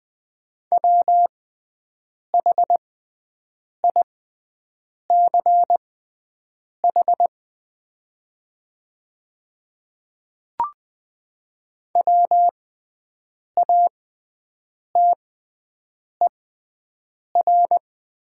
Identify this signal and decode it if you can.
Morse code, heard as a keyed tone, about 20 wpm, spaced as for 7 wpm, 700 Hz. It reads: WHICH WATER
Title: 6x Character Spacing - Top 100 Words - 20wpm